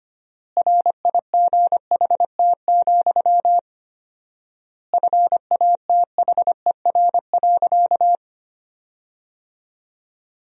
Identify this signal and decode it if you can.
Morse code, heard as a keyed tone, about 25 wpm, 700 Hz. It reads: RIGHT, FATHER.
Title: 4x Word-Spacing - Sentences from Top 300 Words - 25wpm